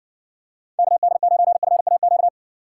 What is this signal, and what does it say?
Morse code, heard as a keyed tone, about 60 wpm, 700 Hz. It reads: KD9FAQ